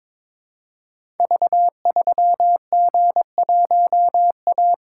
Morse code, heard as a keyed tone, about 22 wpm, 700 Hz. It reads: V3G1A
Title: Code Groups - 5 Letter:Numbers - 22wpm